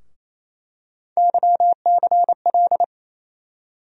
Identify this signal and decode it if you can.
Morse code, heard as a keyed tone, about 28 wpm, 700 Hz. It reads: YCL
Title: USA Call Sign Postfixes - 3 Letters 28wpm